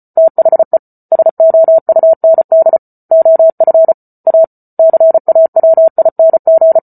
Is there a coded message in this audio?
Yes: Morse code from 0.2 to 6.8 s, heard as a keyed tone, about 34 wpm, 650 Hz. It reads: THE SOUND OF A CAWING